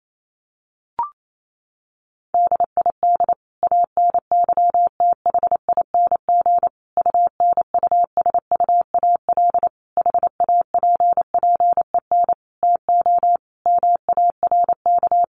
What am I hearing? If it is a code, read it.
Morse code, heard as a keyed tone, about 28 wpm, 700 Hz. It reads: DID ANYTHING UNUSUAL HAPPEN TO MARK
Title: Sentences from Top 1400 Words 28wpm